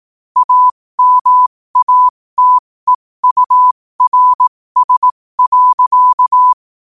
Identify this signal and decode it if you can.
Morse code, heard as a keyed tone, about 18 wpm, spaced as for 15 wpm, 1000 Hz. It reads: AMATEURS.
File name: qotdfiles / 15wpm